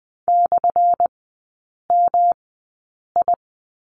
Morse code, heard as a keyed tone, about 20 wpm, spaced as for 8 wpm, 700 Hz. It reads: /MI
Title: 205 Lesson 33e - Letter j - Three Characters 20wpm